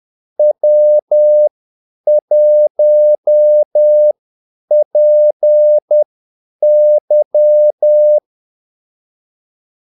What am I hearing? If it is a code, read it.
Morse code, heard as a keyed tone, about 10 wpm, spaced as for 8 wpm, 600 Hz. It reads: W1PY
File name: Callsigns-10-8wpm